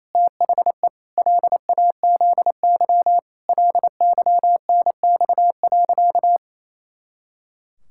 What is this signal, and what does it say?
Morse code, heard as a keyed tone, about 28 wpm, 700 Hz. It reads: THE LAZY LYNX.
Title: Pangram Challenge v2 - 28wpm